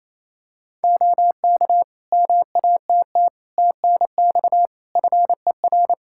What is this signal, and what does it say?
Morse code, heard as a keyed tone, about 28 wpm, 700 Hz. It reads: OK MATT TNX FER
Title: Practice QSO - Exchange 4 - Ending - 28wpm